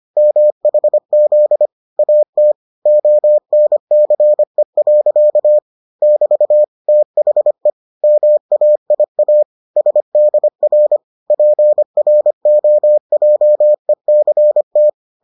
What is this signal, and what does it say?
Morse code, heard as a keyed tone, about 25 wpm, 600 Hz. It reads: MHZ AT ONCE. <BT> THE MAIA SDR PROJECT